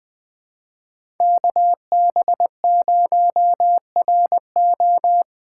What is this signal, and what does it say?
Morse code, heard as a keyed tone, about 19 wpm, 700 Hz. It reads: KB0RO
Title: Speed Racing - USA 2x2 Call Signs 17wpm